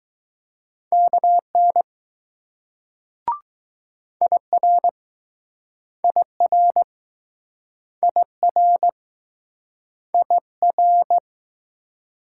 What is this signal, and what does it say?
Morse code, heard as a keyed tone, about 21 wpm, 700 Hz. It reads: KN IR IR IR IR